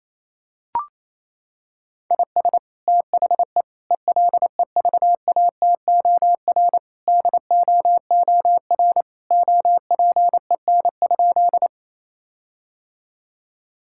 Morse code, heard as a keyed tone, about 28 wpm, 700 Hz. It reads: IS THE ELEVATOR DOOR OPEN?
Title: Sentences from Top 1600 Words 28wpm